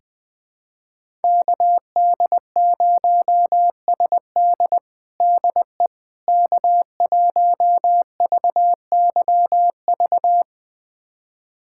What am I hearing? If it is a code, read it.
Morse code, heard as a keyed tone, about 20 wpm, 690 Hz. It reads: KD0SD DE K1VYV
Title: USA Callsign DE Callsign 20wpm